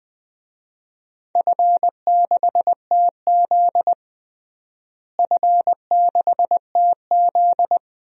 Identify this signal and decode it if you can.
Morse code, heard as a keyed tone, about 20 wpm, 700 Hz. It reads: F6TZ F6TZ
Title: DXCC-100 Call Signs - 4 Characters - Repeated 2x - 20wpm